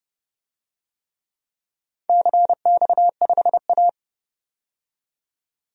Morse code, heard as a keyed tone, about 30 wpm, 700 Hz. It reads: CX5A
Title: DXCC-100 Call Signs - 4 Characters - 30wpm